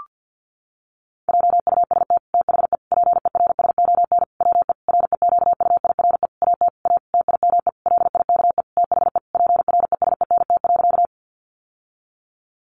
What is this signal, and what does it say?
Morse code, heard as a keyed tone, about 49 wpm, 700 Hz. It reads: JUST THE PERSON WE REQUIRE AT A TIME LIKE THE PRESENT.
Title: Sentences from Top 900 Words 50wpm